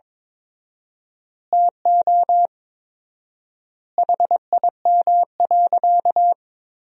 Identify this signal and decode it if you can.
Morse code, heard as a keyed tone, about 22 wpm, 700 Hz. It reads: TO HIM.
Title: 4x Word-Spacing - Sentences from Top 200 Words - 22wpm